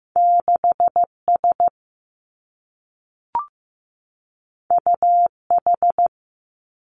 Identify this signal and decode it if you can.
Morse code, heard as a keyed tone, about 15 wpm, 700 Hz. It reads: 6 S U H
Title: Two Challenging Letter-Number Combinations 15wpm